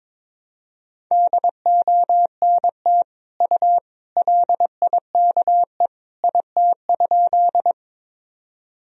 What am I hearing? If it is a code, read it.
Morse code, heard as a keyed tone, about 22 wpm, 700 Hz. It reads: DONT U LIKE IT?
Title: Sentences from Top 300 Words and Abbreviations 22wpm